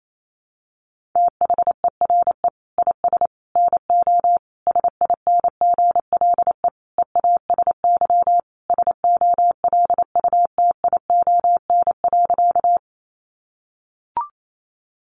Morse code, heard as a keyed tone, about 28 wpm, 700 Hz. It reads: THERE IS NO SINGLE EASY SOLUTION.
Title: Sentences from Top 1300 Words 28wpm